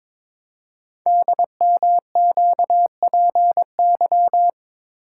Morse code, heard as a keyed tone, about 22 wpm, 700 Hz. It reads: DMQPY
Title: Code Groups - 5 Letters - 22wpm